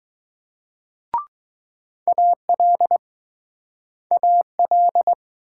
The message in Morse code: AL AL

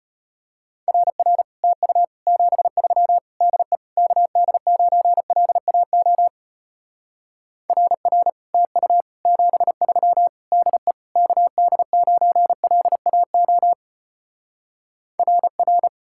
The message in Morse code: RR TU 73 DE KD9LAO RR TU 73 DE KD9LAO RR

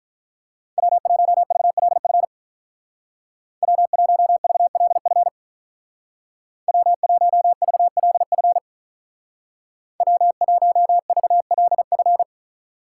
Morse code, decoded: W1VLF W1VLF W1VLF W1VLF

